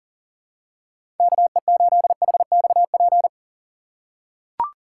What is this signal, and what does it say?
Morse code, heard as a keyed tone, about 40 wpm, 700 Hz. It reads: KE8HXP
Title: USA Call Signs - 6 Characters - 6x Character Spacing - 40wpm